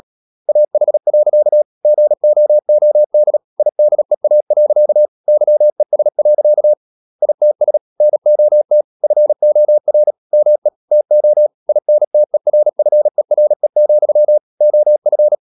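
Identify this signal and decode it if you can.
Morse code, heard as a keyed tone, about 37 wpm, 600 Hz. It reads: AH. GOOD IDEA. YES. ITS NOT FOR ME TO INTERFERE, OF
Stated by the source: Morse code practice at 35–39 wpm sending English text of Ken Follett's novel Never, from CW med Gnister.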